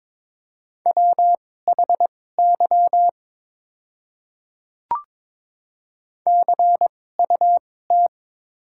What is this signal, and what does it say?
Morse code, heard as a keyed tone, about 22 wpm, spaced as for 16 wpm, 690 Hz. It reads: WHY CUT
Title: 2x Character Spacing - Top 500 Words - 22wpm